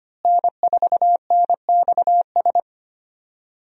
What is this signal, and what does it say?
Morse code, heard as a keyed tone, about 25 wpm, 700 Hz. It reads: N4NXS
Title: USA Call Signs - 5 Characters - Repeated 2x - 25wpm